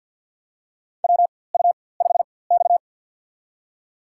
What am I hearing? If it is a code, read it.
Morse code, heard as a keyed tone, about 47 wpm, spaced as for 23 wpm, 700 Hz. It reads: WU5X
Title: USA Call Signs - 4x Character Spacing - 50wpm